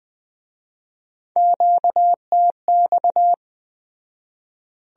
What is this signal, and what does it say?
Morse code, heard as a keyed tone, about 21 wpm, 700 Hz. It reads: QTX